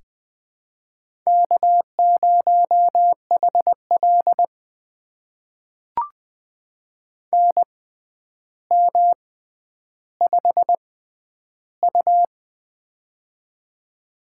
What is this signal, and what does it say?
Morse code, heard as a keyed tone, about 20 wpm, spaced as for 10 wpm, 700 Hz. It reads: K0HL NM5U
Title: USA Call Signs - 4 Characters - 6x Character Spacing - 20wpm